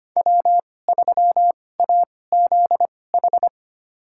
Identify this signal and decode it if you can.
Morse code, heard as a keyed tone, about 25 wpm, spaced as for 18 wpm, 700 Hz. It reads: W3AZH
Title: USA Call Signs - 5 Characters - 2x Character Spacing - 25wpm